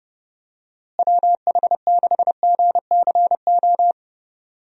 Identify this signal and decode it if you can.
Morse code, heard as a keyed tone, about 31 wpm, 700 Hz. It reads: WH6GCO
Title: USA Call Signs - 6 Characters - Repeated 3x - 30wpm